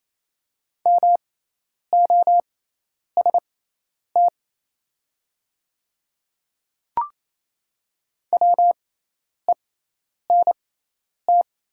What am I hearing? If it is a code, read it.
Morse code, heard as a keyed tone, about 28 wpm, spaced as for 10 wpm, 700 Hz. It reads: MOST WENT
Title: 6x Character Spacing - Top 200 Words - 28wpm